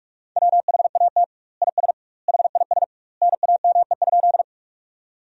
Morse code, heard as a keyed tone, about 46 wpm, 700 Hz. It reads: WHAT IS HIS NAME?
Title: 2x Word-Spacing - Sentences from Top 200 Words - 45wpm